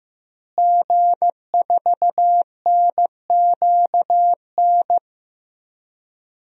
Morse code, heard as a keyed tone, about 15 wpm, 700 Hz. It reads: G4NQN
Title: DXCC-100 Call Signs - 5 Characters - Repeated 3x - 15wpm